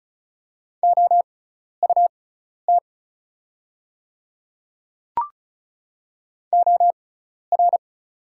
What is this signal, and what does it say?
Morse code, heard as a keyed tone, about 36 wpm, spaced as for 12 wpm, 690 Hz. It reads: OUT OR
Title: 6x Character Spacing - Top 100 Words - 35wpm